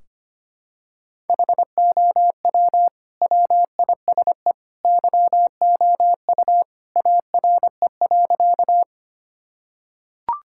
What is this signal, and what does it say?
Morse code, heard as a keyed tone, about 25 wpm, 700 Hz. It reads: HOW WISE YOU ARE.
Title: Sentences from Top 1600 Words 25wpm